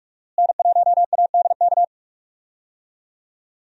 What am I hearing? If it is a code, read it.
Morse code, heard as a keyed tone, about 42 wpm, 700 Hz. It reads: N1ADK
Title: USA Call Signs - 5 Characters - Repeated 3x - 45wpm